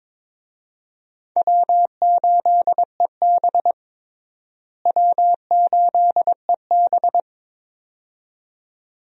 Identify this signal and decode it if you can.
Morse code, heard as a keyed tone, about 22 wpm, 700 Hz. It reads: W8EB W8EB
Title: USA Call Signs - 4 Characters - Repeated 2x - 22wpm